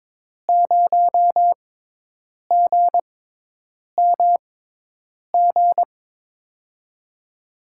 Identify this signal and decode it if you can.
Morse code, heard as a keyed tone, about 21 wpm, spaced as for 8 wpm, 700 Hz. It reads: 0GMG
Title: USA Call Signs - 6x Character Spacing - 22wpm